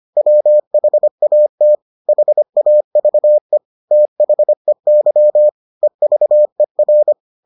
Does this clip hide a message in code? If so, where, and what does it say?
Yes: Morse code from 0.2 to 7.1 s, heard as a keyed tone, about 25 wpm, 600 Hz. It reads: WHAT HAVE THEY EVER